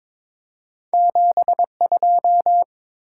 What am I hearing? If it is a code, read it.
Morse code, heard as a keyed tone, about 22 wpm, 700 Hz. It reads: 72
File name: practice / 22wpm / Two Letter-Number Combinations - 22wpm